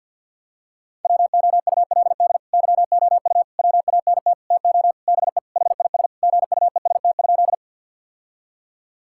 Morse code, heard as a keyed tone, about 50 wpm, 700 Hz. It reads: WOULD YOU WANT TO BE HIS GUEST?
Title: Sentences from Top 1600 Words 50wpm